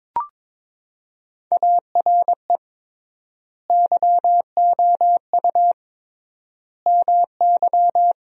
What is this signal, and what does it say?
Morse code, heard as a keyed tone, about 21 wpm, 700 Hz. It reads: ARE YOU MY
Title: 3x Word-Spacing - Sentences from Top 300 Words - 22wpm